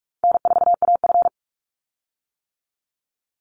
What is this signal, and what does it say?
Morse code, heard as a keyed tone, about 44 wpm, 700 Hz. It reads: N4AF
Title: Easy Call Signs - 45wpm